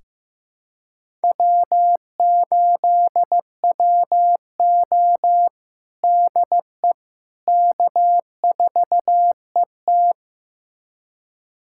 Morse code, heard as a keyed tone, about 15 wpm, 700 Hz. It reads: W8WO DE K4ET
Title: USA Callsign DE Callsign - 4 Characters - 15wpm